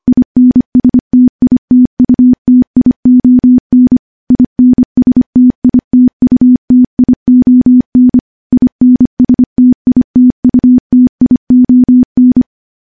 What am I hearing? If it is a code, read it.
Morse code, heard as a keyed tone, about 25 wpm, 250 Hz. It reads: INSTITUTION INSTITUTION INSTITUTION
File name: morse / morse-training-25WPM